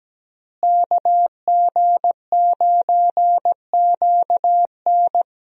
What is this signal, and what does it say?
Morse code, heard as a keyed tone, about 17 wpm, 700 Hz. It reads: KG9QN
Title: USA Call Signs - 5 Characters - Repeated 2x - 17wpm